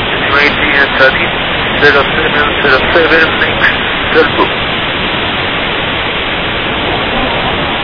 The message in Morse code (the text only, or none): none